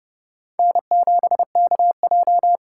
Morse code, heard as a keyed tone, about 29 wpm, 700 Hz. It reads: N7KJ